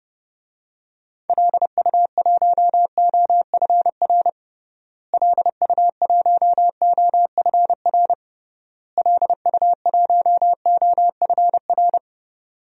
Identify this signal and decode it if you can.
Morse code, heard as a keyed tone, about 30 wpm, 700 Hz. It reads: LU1OFR LU1OFR LU1OFR
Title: DXCC-100 Call Signs - 6 Characters - Repeated 3x - 30wpm